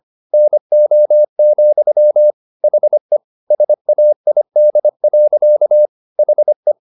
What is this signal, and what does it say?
Morse code, heard as a keyed tone, about 25 wpm, 600 Hz. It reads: NO, HE SAID. HE